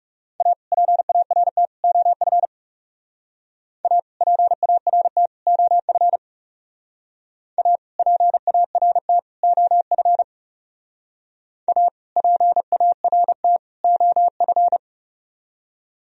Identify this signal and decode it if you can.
Morse code, heard as a keyed tone, about 36 wpm, 700 Hz. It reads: A PART OF A PART OF A PART OF A PART OF
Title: Speed-Racing - Sets of 3 Words 30wpm